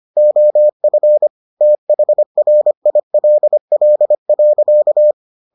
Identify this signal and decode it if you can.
Morse code, heard as a keyed tone, about 25 wpm, 600 Hz. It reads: OF THRILL.